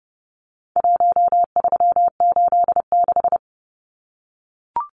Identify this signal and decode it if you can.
Morse code, heard as a keyed tone, about 30 wpm, 700 Hz. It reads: 1386
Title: Numbers - Contest Style - 4-Digit 30wpm